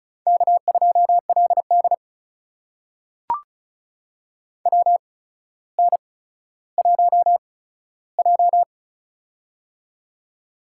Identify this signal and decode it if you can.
Morse code, heard as a keyed tone, about 36 wpm, spaced as for 15 wpm, 700 Hz. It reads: K2LD WN1J
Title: USA Call Signs - 8x Character Spacing - 35wpm